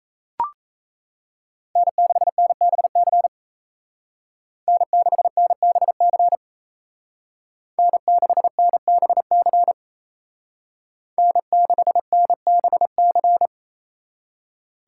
Morse code, heard as a keyed tone, about 36 wpm, 700 Hz. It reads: N6NBC N6NBC N6NBC N6NBC